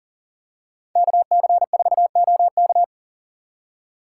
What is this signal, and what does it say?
Morse code, heard as a keyed tone, about 40 wpm, 700 Hz. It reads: KC4OK
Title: USA Call Signs - 5 Characters - 4x Character Spacing - 40wpm